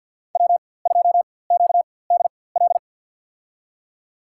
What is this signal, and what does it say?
Morse code, heard as a keyed tone, about 52 wpm, spaced as for 23 wpm, 700 Hz. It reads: W2QDL
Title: USA Call Signs - 4x Character Spacing - 50wpm